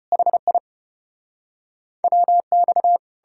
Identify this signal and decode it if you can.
Morse code, heard as a keyed tone, about 38 wpm, 700 Hz. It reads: HI WX